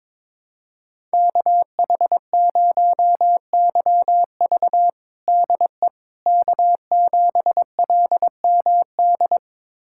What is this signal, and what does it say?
Morse code, heard as a keyed tone, about 22 wpm, 700 Hz. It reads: KH0YV DE K7LMD
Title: USA Callsign DE Callsign - 5 Characters - 22wpm